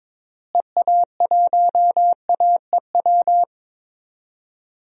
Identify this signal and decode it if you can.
Morse code, heard as a keyed tone, about 22 wpm, 700 Hz. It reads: EA1AEW
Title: DXCC-100 Call Signs - 6 Characters - Repeated 3x - 22wpm